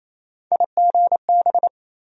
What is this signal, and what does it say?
Morse code, heard as a keyed tone, about 28 wpm, 700 Hz. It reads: IGB
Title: USA Call Sign Postfixes - 3 Letters 28wpm